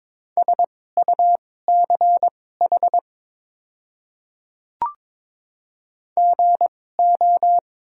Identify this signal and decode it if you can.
Morse code, heard as a keyed tone, about 22 wpm, spaced as for 16 wpm, 690 Hz. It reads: SUCH GO
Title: 2x Character Spacing - Top 500 Words - 22wpm